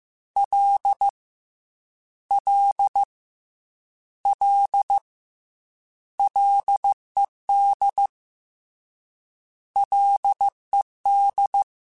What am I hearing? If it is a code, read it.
Morse code, heard as a keyed tone, about 15 wpm, 800 Hz. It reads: L L L LED LED
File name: Ls / Single L m3s / Single L @13